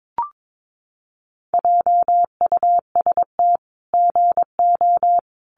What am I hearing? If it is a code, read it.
Morse code, heard as a keyed tone, about 22 wpm, 700 Hz. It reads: JUST GO
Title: Sentences from Top 500 Words 22wpm